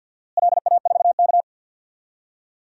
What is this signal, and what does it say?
Morse code, heard as a keyed tone, about 47 wpm, 700 Hz. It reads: LA4K
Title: DXCC-100 Call Signs - 4 Characters - Repeated 3x - 50wpm